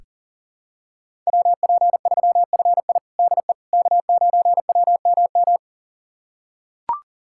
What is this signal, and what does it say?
Morse code, heard as a keyed tone, about 40 wpm, 700 Hz. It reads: WP3FI DE K9WMM
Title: USA Callsign DE Callsign - 5 Characters - 40wpm